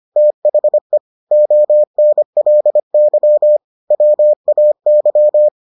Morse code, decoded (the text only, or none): THE ONLY WAY